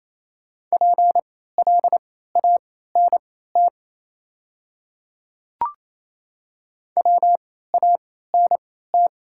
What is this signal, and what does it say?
Morse code, heard as a keyed tone, about 28 wpm, spaced as for 16 wpm, 700 Hz. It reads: PLANT WANT
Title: 3x Character Spacing - Top 500 Words - 28wpm